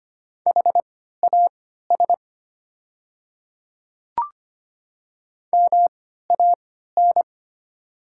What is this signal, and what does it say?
Morse code, heard as a keyed tone, about 25 wpm, spaced as for 14 wpm, 700 Hz. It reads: HAS MAN
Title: 3x Character Spacing - Top 200 Words - 25wpm